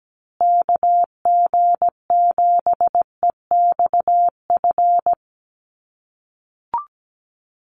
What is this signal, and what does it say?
Morse code, heard as a keyed tone, about 17 wpm, 700 Hz. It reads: KG7EXF
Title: USA Call Signs Repeated 3x 17wpm